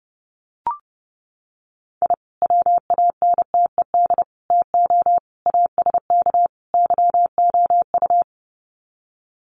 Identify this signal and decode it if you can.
Morse code, heard as a keyed tone, about 30 wpm, 700 Hz. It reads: I WANTED TO ASK YOU
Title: Sets of 5 Words Encore2 30wpm